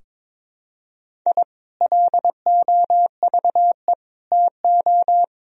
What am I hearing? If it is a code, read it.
Morse code, heard as a keyed tone, about 22 wpm, 700 Hz. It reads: I LOVE TO
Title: Sentences from Top 1000 Words - Review Part 4 - 22wpm